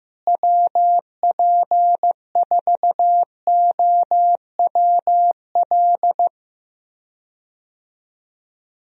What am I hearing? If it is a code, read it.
Morse code, heard as a keyed tone, about 15 wpm, 700 Hz. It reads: WP4OWL
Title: USA Call Signs - 6 Characters - Repeated 3x - 15wpm